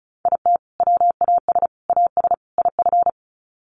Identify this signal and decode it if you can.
Morse code, heard as a keyed tone, about 34 wpm, 700 Hz. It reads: IT WAS AS IF
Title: Sets of 4 Words 35wpm